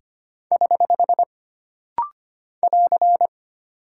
Morse code, heard as a keyed tone, about 25 wpm, 700 Hz. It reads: <HH> <AR>
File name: Single Punctuation-Prosign - Rapid-Fire - 25wpm